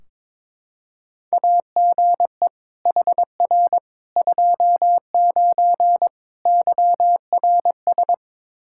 Morse code, heard as a keyed tone, about 21 wpm, 700 Hz. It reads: AGE HR 29 YRS
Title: Practice QSO - Exchange 3 - More Info - 22wpm